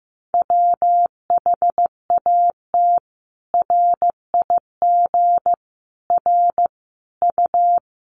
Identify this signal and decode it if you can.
Morse code, heard as a keyed tone, about 15 wpm, 700 Hz. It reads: WHAT RIG R U
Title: Common QSO Phrases 15wpm